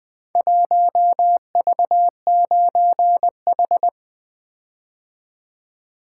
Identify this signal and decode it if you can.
Morse code, heard as a keyed tone, about 20 wpm, 700 Hz. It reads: 1V9H
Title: Four Letter-Number Combinations - 20wpm